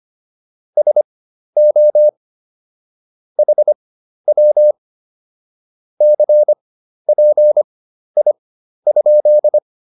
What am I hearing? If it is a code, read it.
Morse code, heard as a keyed tone, about 25 wpm, spaced as for 12 wpm, 600 Hz. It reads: SO HW CPI?
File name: QSO 204 12